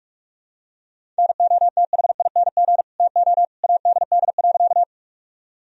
Morse code, heard as a keyed tone, about 41 wpm, 700 Hz. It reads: NOTHING TO ADD.
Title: Sentences from Top 1000 Words - Review Part 1 - 45wpm